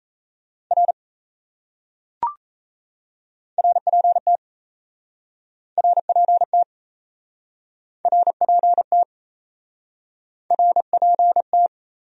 Morse code, heard as a keyed tone, about 37 wpm, 700 Hz. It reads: R RPT RPT RPT RPT